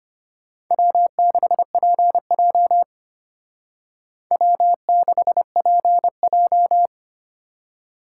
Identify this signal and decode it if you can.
Morse code, heard as a keyed tone, about 32 wpm, 700 Hz. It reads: W6PJ W6PJ